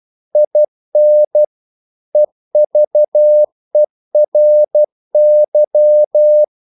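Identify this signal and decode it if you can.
Morse code, heard as a keyed tone, about 12 wpm, 600 Hz. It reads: IN EVERY
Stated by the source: Morse practice sending English quotations.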